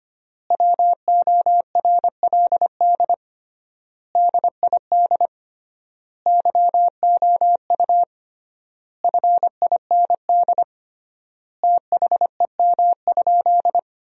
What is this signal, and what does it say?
Morse code, heard as a keyed tone, about 25 wpm, 700 Hz. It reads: WORLD DID YOU FIND THEM?
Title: 3x Word-Spacing - Sentences from Top 200 Words - 25wpm